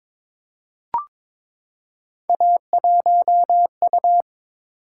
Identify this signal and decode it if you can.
Morse code, heard as a keyed tone, about 22 wpm, 700 Hz. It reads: A1U
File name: practice / 22wpm / Three Letter-Number Combinations - 22wpm